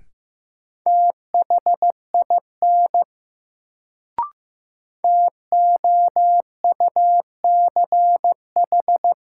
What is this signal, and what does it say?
Morse code, heard as a keyed tone, about 15 wpm, 700 Hz. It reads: THIN TOUCH
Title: Sentences from Top 1200 Words - New Word Drill - 15wpm